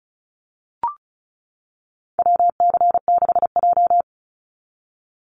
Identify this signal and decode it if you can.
Morse code, heard as a keyed tone, about 34 wpm, 700 Hz. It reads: WC6J
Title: NA SOTA Activator Callsigns 35wpm